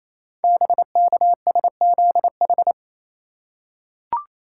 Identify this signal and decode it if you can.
Morse code, heard as a keyed tone, about 28 wpm, 690 Hz. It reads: BKSZH
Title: Code Groups - 5 Letters - 28wpm